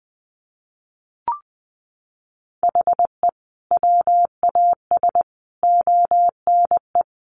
Morse code, heard as a keyed tone, about 19 wpm, 700 Hz. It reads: HE WAS ONE
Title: Sets of 5 Words 20wpm